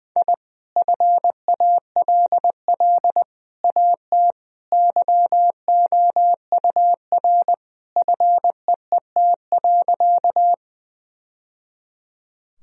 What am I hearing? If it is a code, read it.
Morse code, heard as a keyed tone, about 20 wpm, 700 Hz. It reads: I FALL AT YOUR FEET.
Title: Sentences from Top 1000 Words - Review Part 2 - 20wpm